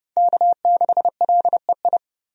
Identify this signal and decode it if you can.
Morse code, heard as a keyed tone, about 30 wpm, 700 Hz. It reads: K6LEI
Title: USA Call Signs - 5 Characters - 30wpm